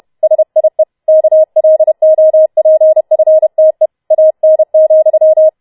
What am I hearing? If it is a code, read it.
Morse code, heard as a keyed tone, about 31 wpm, 630 Hz. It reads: SIE KLOPFTE AN,